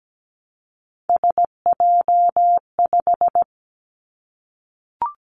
Morse code, heard as a keyed tone, about 17 wpm, 700 Hz. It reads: SJ5
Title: Three Challenging Letter-Number Combinations 17wpm